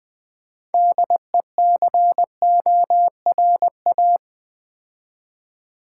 Morse code, heard as a keyed tone, about 20 wpm, 700 Hz. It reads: DECORA
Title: Build A Word - 3 Letters At A Time - 8-Letter Words - 20wpm